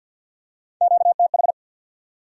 Morse code, heard as a keyed tone, about 48 wpm, 700 Hz. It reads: QTH
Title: Top 180 QSO Elements 50wpm